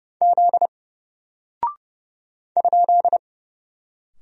Morse code, heard as a keyed tone, about 30 wpm, 700 Hz. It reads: Z ?
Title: Single Character (Letter:Number:Punctuation:Prosign) Alt 30wpm